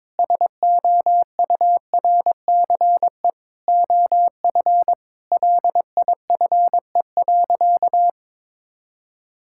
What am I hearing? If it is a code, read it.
Morse code, heard as a keyed tone, about 22 wpm, 700 Hz. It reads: SOURCE OF LIFE.
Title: Sentences from Top 1000 Words - Review Part 5 - 22wpm